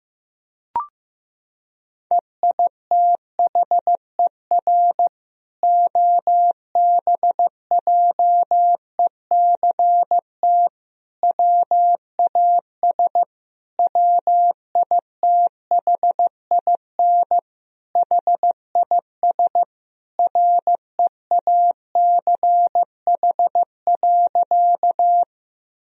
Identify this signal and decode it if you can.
Morse code, heard as a keyed tone, about 15 wpm, 700 Hz. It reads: EITHER OBJECT WAS WITHIN HIS REACH.
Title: Sentences from Top 1000 Words - Review Part 4 - 15wpm